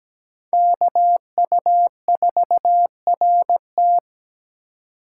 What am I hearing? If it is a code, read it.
Morse code, heard as a keyed tone, about 20 wpm, 700 Hz. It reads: KU4RT
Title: Speed Racing - USA 2x2 Call Signs 17wpm